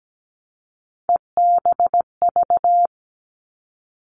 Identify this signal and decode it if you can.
Morse code, heard as a keyed tone, about 17 wpm, 700 Hz. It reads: EBV
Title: All Cut Numbers - 3-Digit 17wpm